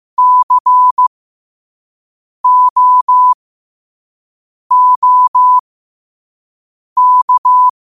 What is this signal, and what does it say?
Morse code, heard as a keyed tone, about 15 wpm, spaced as for 5 wpm, 1000 Hz. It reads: COOK